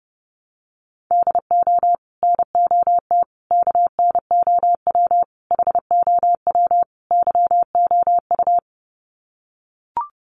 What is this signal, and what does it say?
Morse code, heard as a keyed tone, about 30 wpm, 700 Hz. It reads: DO NOT KNOW HOW YOU
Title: Sets of 5 Words Encore 30wpm